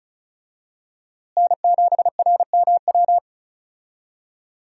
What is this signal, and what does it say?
Morse code, heard as a keyed tone, about 34 wpm, 700 Hz. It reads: N7RMW
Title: USA Call Signs - 5 Characters - Repeated 3x - 35wpm